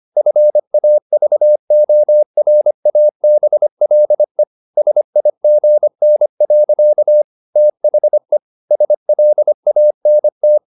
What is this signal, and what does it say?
Morse code, heard as a keyed tone, about 25 wpm, 600 Hz. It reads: FAVORABLE SIGN. THE SLANT
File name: the old man and the sea part 1 25wpm